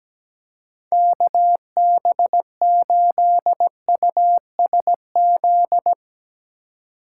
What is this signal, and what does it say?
Morse code, heard as a keyed tone, about 18 wpm, 700 Hz. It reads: KB8USZ
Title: Speed Racing - USA Call Signs 17wpm